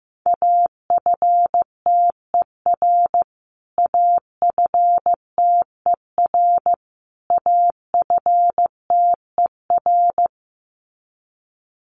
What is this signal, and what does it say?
Morse code, heard as a keyed tone, about 15 wpm, 700 Hz. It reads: AFTER AFTER AFTER
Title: Top 300 Words Repeated 15wpm